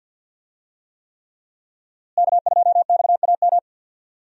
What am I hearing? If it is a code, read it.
Morse code, heard as a keyed tone, about 52 wpm, 700 Hz. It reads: K2XAM
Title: USA Call Signs - 5 Characters - Repeated 3x - 50wpm